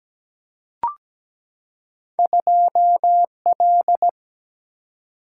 Morse code, heard as a keyed tone, about 17 wpm, 700 Hz. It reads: 2L
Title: Single-Letter Single-Number 17wpm